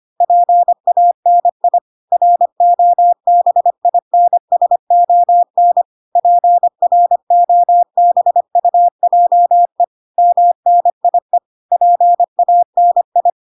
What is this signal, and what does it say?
Morse code, heard as a keyed tone, about 25 wpm, 700 Hz. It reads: PANI ROBINSON PROBUJE MNIE PANI